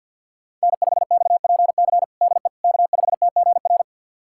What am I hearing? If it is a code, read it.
Morse code, heard as a keyed tone, about 47 wpm, 700 Hz. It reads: N5XPC DE K5TGR
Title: USA Callsign DE Callsign - 5 Characters - 50wpm